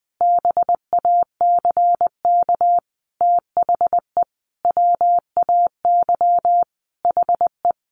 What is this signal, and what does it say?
Morse code, heard as a keyed tone, about 20 wpm, 700 Hz. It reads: BACK THE WAY HE
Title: Sentences from Top 1100 Words 20wpm